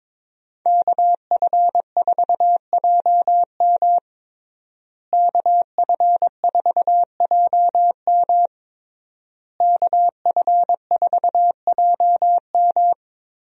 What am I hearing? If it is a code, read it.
Morse code, heard as a keyed tone, about 22 wpm, 700 Hz. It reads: KF4JM KF4JM KF4JM
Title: USA Call Signs - 5 Characters - Repeated 3x - 22wpm